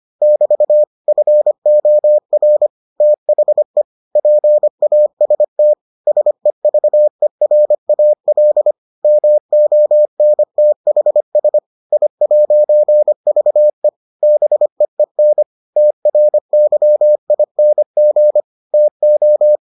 Morse code, heard as a keyed tone, about 25 wpm, 600 Hz. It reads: <BT> FOR THE PAST SEVERAL MONTHS I'VE BEEN TRYING TO